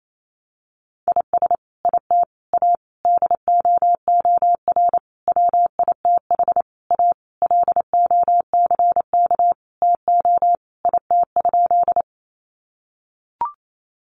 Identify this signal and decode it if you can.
Morse code, heard as a keyed tone, about 28 wpm, 700 Hz. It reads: IS IT A DOOR WITH A LOCK TO IT?
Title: Sentences from Top 1300 Words 28wpm